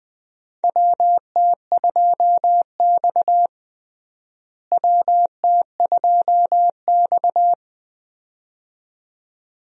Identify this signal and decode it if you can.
Morse code, heard as a keyed tone, about 20 wpm, 700 Hz. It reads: WT2X WT2X
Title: USA Call Signs - 4 Characters - Repeated 2x - 20wpm